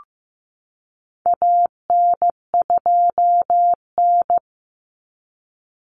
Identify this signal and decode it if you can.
Morse code, heard as a keyed tone, about 15 wpm, 700 Hz. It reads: AN2N
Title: Common Cut Numbers - 4-Digit 15wpm